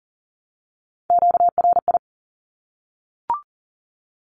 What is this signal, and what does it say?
Morse code, heard as a keyed tone, about 40 wpm, 700 Hz. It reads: QRI
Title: Q-Codes Repeated 3x 40wpm